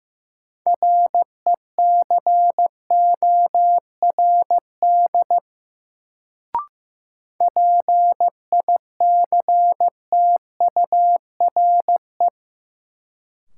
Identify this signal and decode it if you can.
Morse code, heard as a keyed tone, about 15 wpm, 700 Hz. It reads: RECORD PICTURE
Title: Sentences from Top 500 Words - New Word Drill - 15wpm